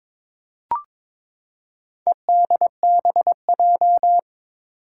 Morse code, heard as a keyed tone, about 22 wpm, 700 Hz. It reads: EDBJ
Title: Four Challenging Letter-Number Combinations 22wpm